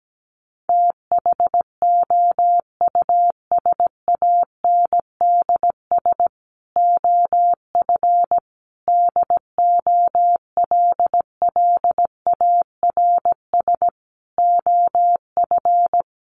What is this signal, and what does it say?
Morse code, heard as a keyed tone, about 17 wpm, 700 Hz. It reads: THOUSANDS OF DOLLARS OF
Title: A Sound Of Thunder 17wpm